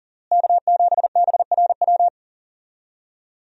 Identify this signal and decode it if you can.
Morse code, heard as a keyed tone, about 40 wpm, 700 Hz. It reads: K7BRW
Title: USA Call Signs - 5 Characters - Repeated 3x - 40wpm